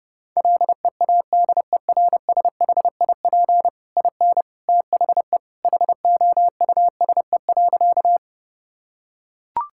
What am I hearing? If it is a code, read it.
Morse code, heard as a keyed tone, about 32 wpm, 700 Hz. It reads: LEADERSHIP IN THE HOUSE.
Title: Sentences from Top 1600 Words 30wpm